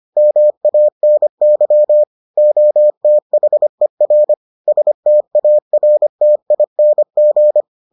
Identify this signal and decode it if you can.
Morse code, heard as a keyed tone, about 25 wpm, 600 Hz. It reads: MANY OTHER STARTING